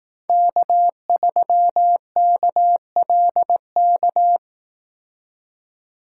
Morse code, heard as a keyed tone, about 18 wpm, 700 Hz. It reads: K3KLK
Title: Speed Racing - USA 1x3 Call Signs 15wpm